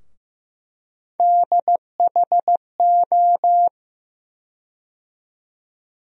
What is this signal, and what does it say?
Morse code, heard as a keyed tone, about 15 wpm, 700 Hz. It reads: DHO
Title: USA Call Sign Postfixes - 3 Letters 15wpm